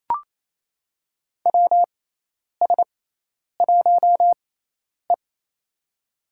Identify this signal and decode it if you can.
Morse code, heard as a keyed tone, about 28 wpm, spaced as for 10 wpm, 700 Hz. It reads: WS1E
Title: USA Call Signs - 4 Characters - 6x Character Spacing - 28wpm